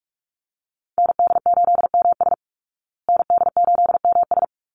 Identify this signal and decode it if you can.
Morse code, heard as a keyed tone, about 44 wpm, 700 Hz. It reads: ND8MS ND8MS
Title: Call Signs Repeated 3x - 45wpm